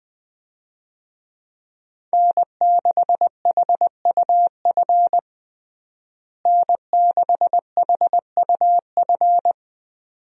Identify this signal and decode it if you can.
Morse code, heard as a keyed tone, about 20 wpm, 700 Hz. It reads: N6HUF N6HUF
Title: USA Call Signs - 5 Characters - Repeated 2x - 20wpm